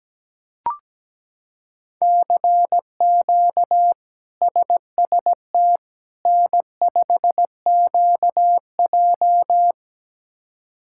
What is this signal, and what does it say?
Morse code, heard as a keyed tone, about 17 wpm, 700 Hz. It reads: CQ SST N5QJ
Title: K1USN SST QSO Practice - 17wpm